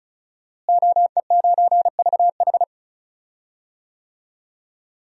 Morse code, heard as a keyed tone, about 34 wpm, 700 Hz. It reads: OE9VH
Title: DXCC-100 Call Signs - 5 Characters - Repeated 3x - 35wpm